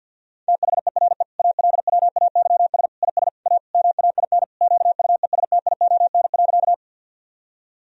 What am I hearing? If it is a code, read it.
Morse code, heard as a keyed tone, about 53 wpm, 700 Hz. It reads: THERE ALWAYS IS A MAIN QUESTION.
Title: Sentences from Top 1000 Words - Review Part 2 - 50wpm